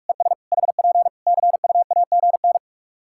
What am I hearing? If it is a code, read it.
Morse code, heard as a keyed tone, about 43 wpm, 700 Hz. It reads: ES HP CUAGN
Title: Practice QSO - Exchange 4 - Ending - 45wpm